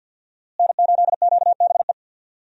Morse code, heard as a keyed tone, about 51 wpm, 700 Hz. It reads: N8QBE
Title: USA Call Signs - 5 Characters - Repeated 2x - 50wpm